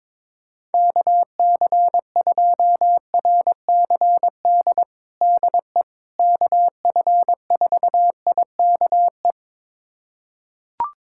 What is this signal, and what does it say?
Morse code, heard as a keyed tone, about 22 wpm, 700 Hz. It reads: KC2RCD DE KF4IKE
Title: USA Callsign DE Callsign 22wpm